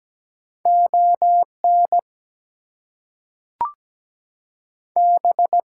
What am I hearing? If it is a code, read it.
Morse code, heard as a keyed tone, about 17 wpm, 700 Hz. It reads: ON B
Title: Two Letter Words - 17wpm